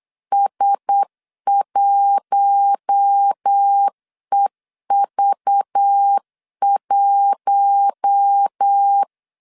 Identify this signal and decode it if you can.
Morse code, heard as a keyed tone, about 8 wpm, 800 Hz. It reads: S1EV1
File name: CW / DET / 1 6 1500C S B 8wpm 5142